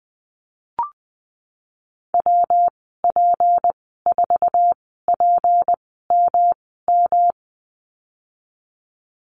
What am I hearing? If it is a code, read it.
Morse code, heard as a keyed tone, about 20 wpm, spaced as for 15 wpm, 700 Hz. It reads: WP4PMM
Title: 167 Lesson 27g - Number 7 - Call Signs 20wpm